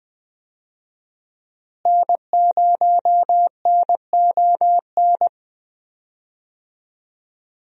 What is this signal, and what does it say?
Morse code, heard as a keyed tone, about 20 wpm, 700 Hz. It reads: N0NON
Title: USA Call Signs - 5 Characters - Repeated 3x - 20wpm